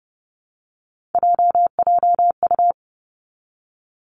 Morse code, heard as a keyed tone, about 30 wpm, 700 Hz. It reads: JJU
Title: Three Challenging Letter-Number Combinations 30wpm